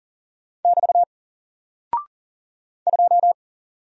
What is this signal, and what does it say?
Morse code, heard as a keyed tone, about 40 wpm, 700 Hz. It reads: <BT> 2
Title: Single Character (Letter:Number:Punctuation:Prosign) 40wpm